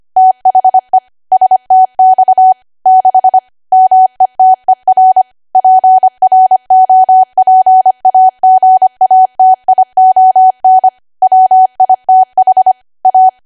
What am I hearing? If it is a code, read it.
Morse code, heard as a keyed tone, about 25 wpm, 750 Hz. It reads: THE STX 6 METER PROPAGATION WITH A
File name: cwa / W1AW-Level3 / 25WPM